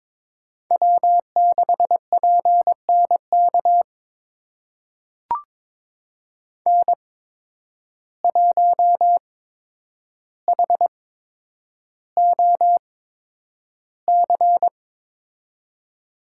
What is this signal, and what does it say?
Morse code, heard as a keyed tone, about 22 wpm, spaced as for 9 wpm, 700 Hz. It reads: W6PNK N1HOC
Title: USA Call Signs - 5 Characters - 8x Character Spacing - 22wpm